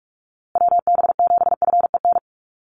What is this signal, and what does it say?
Morse code, heard as a keyed tone, about 44 wpm, 700 Hz. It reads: WB7FEN
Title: USA Call Signs Repeated 3x 45wpm